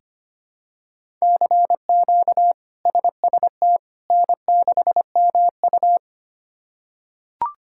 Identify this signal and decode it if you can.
Morse code, heard as a keyed tone, about 25 wpm, 700 Hz. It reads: CQ SST N6MU
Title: K1USN SST QSO Practice - 25wpm